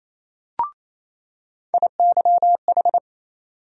Morse code, heard as a keyed tone, about 28 wpm, 700 Hz. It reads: IYH
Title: USA Call Sign Postfixes - 3 Letters 28wpm